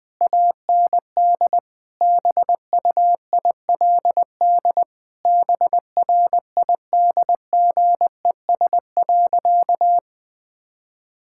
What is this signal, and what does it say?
Morse code, heard as a keyed tone, about 20 wpm, 700 Hz. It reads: AND BUILD BRIDGES.
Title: Difficult Sentences with Hints - 20wpm